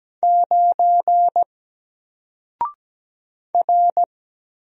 Morse code, heard as a keyed tone, about 17 wpm, 700 Hz. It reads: 9 R
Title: Single Letter-Number Alt 17wpm